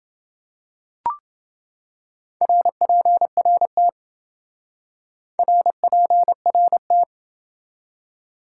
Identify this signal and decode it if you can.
Morse code, heard as a keyed tone, about 27 wpm, 700 Hz. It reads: RPRT RPRT